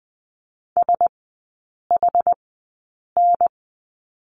Morse code, heard as a keyed tone, about 20 wpm, spaced as for 8 wpm, 700 Hz. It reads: SHN
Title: 074 Lesson 13e - Letter L - Three Characters 20wpm